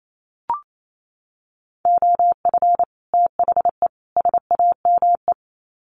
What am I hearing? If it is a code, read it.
Morse code, heard as a keyed tone, about 28 wpm, 700 Hz. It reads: OF THE SAME
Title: Sets of 3 Words 28wpm